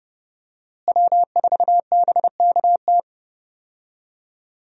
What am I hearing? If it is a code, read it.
Morse code, heard as a keyed tone, about 30 wpm, 700 Hz. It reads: W4BKT
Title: USA Call Signs - 5 Characters - Repeated 2x - 30wpm